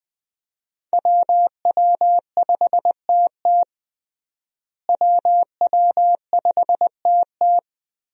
Morse code, heard as a keyed tone, about 20 wpm, 700 Hz. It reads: WW5TT WW5TT
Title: USA Call Signs - 5 Characters - Repeated 3x - 20wpm